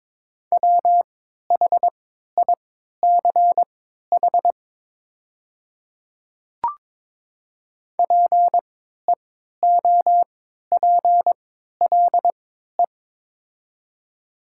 Morse code, heard as a keyed tone, about 22 wpm, spaced as for 12 wpm, 700 Hz. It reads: WHICH PEOPLE
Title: 3x Character Spacing - Top 200 Words - 22wpm